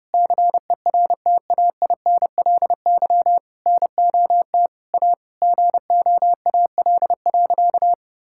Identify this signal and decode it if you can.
Morse code, heard as a keyed tone, about 30 wpm, 700 Hz. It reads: CERTAINLY NOT A GOAL.
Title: Sentences from Top 1000 Words - Review Part 5 - 30wpm